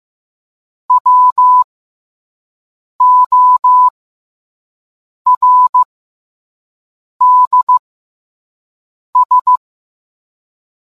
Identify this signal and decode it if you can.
Morse code, heard as a keyed tone, about 15 wpm, spaced as for 5 wpm, 1000 Hz. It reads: WORDS